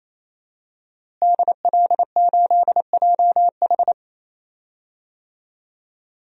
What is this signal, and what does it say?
Morse code, heard as a keyed tone, about 28 wpm, 700 Hz. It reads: DL8JH
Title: DXCC-100 Call Signs - 5 Characters - Repeated 3x - 28wpm